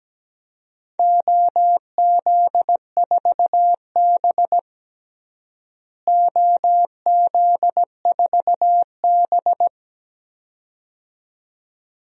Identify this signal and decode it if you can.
Morse code, heard as a keyed tone, about 17 wpm, 700 Hz. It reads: OZ4B OZ4B
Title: DXCC-100 Call Signs - 4 Characters - Repeated 2x - 17wpm